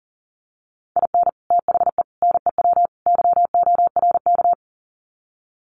Morse code, heard as a keyed tone, about 40 wpm, 700 Hz. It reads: IN THE NEW YORK
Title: Sets of 4 Words 40wpm